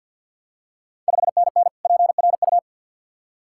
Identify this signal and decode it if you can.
Morse code, heard as a keyed tone, about 48 wpm, 690 Hz. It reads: 5NN PRU
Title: Ontario QSO Party Exchange 50wpm